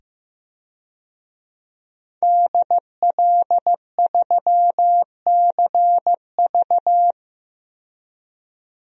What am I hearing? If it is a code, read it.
Morse code, heard as a keyed tone, about 15 wpm, 700 Hz. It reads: DL3CV
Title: DXCC-100 Call Signs - 5 Characters - 15wpm